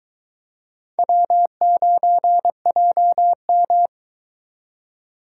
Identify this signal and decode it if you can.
Morse code, heard as a keyed tone, about 22 wpm, 700 Hz. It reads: W9JM